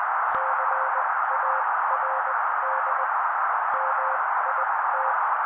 Morse code, heard as a keyed tone, about 20 wpm, spaced as for 16 wpm, 530 Hz. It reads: CARD MIT
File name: daily / 16wpm noise